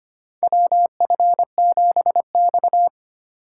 Morse code, heard as a keyed tone, about 25 wpm, 700 Hz. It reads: WF7X
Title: USA Call Signs - 3x Character Spacing - 25wpm